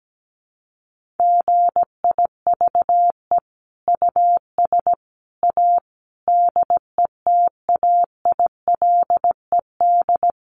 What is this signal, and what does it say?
Morse code, heard as a keyed tone, about 17 wpm, 700 Hz. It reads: GIVE US A DETAILED
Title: Sentences from Top 1500 Words 17wpm